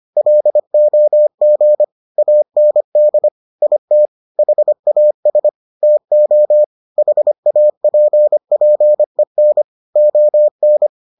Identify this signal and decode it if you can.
Morse code, heard as a keyed tone, about 25 wpm, 600 Hz. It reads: LOG AND IT HAS TO HAPPEN ON